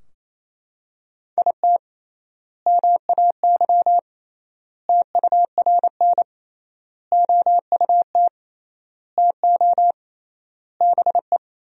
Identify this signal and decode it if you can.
Morse code, heard as a keyed tone, about 28 wpm, 700 Hz. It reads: IT MAY TURN OUT TO BE
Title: 3x Word-Spacing - Sentences from Top 400 Words - 28wpm